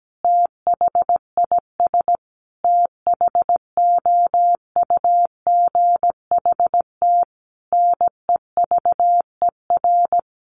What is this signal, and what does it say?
Morse code, heard as a keyed tone, about 17 wpm, 700 Hz. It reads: THIS THOUGHT NEVER
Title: Sentences from Top 900 Words 17wpm